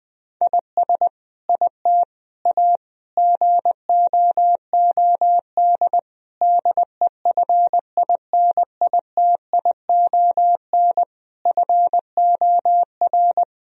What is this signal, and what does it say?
Morse code, heard as a keyed tone, about 20 wpm, 700 Hz. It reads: IS IT A GOOD DEFINITION FOR
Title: Sentences from Top 1000 Words - Review Part 4 - 20wpm